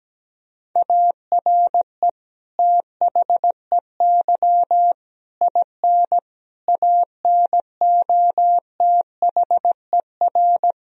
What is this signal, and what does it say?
Morse code, heard as a keyed tone, about 17 wpm, 700 Hz. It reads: ARE THEY IN ANOTHER
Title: Sentences from Top 1000 Words - Review Part 1 - 17wpm